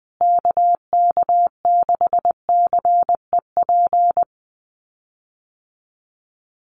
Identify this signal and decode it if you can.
Morse code, heard as a keyed tone, about 20 wpm, 700 Hz. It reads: KK6CEP
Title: USA Call Signs Repeated 3x 20wpm